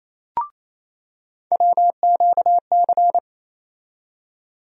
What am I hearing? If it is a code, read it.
Morse code, heard as a keyed tone, about 28 wpm, 700 Hz. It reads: WQC
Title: USA Call Sign Postfixes - 3 Letters 28wpm